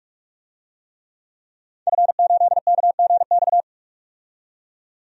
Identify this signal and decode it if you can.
Morse code, heard as a keyed tone, about 46 wpm, 700 Hz. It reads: F8KGX
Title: DXCC-100 Call Signs - 5 Characters - 45wpm